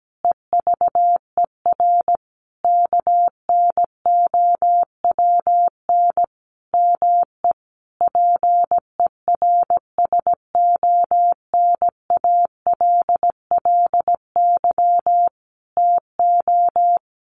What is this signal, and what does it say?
Morse code, heard as a keyed tone, about 17 wpm, 700 Hz. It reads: EVER KNOWN ME PERSONALLY TO
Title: Sentences from Top 1400 Words 17wpm